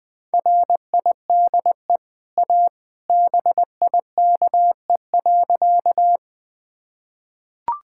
Morse code, heard as a keyed tone, about 20 wpm, 700 Hz. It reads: RIDE A BIKE.
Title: Easy Sentences with Hints - 20wpm